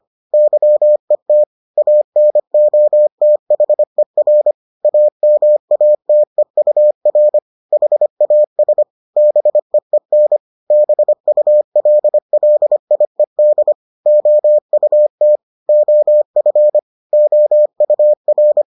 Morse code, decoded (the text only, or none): YET ANOTHER AMATEUR HAS BEEN BULLIED OUT OF OUR